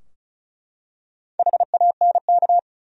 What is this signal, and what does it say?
Morse code, heard as a keyed tone, about 34 wpm, 700 Hz. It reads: HANK
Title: USA Common Names - 35wpm